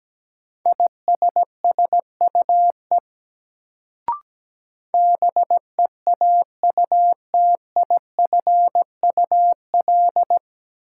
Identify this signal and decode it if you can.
Morse code, heard as a keyed tone, about 17 wpm, 700 Hz. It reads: ISSUE BEAUTIFUL
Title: Sentences from Top 900 Words - New Word Drill - 17wpm